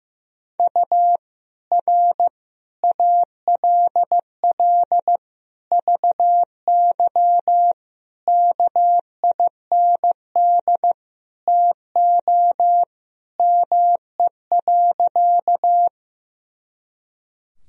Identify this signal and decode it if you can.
Morse code, heard as a keyed tone, about 15 wpm, 700 Hz. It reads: U R ALL VY KIND TO ME.
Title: Sentences from Top 200 Words and Abbreviations 15wpm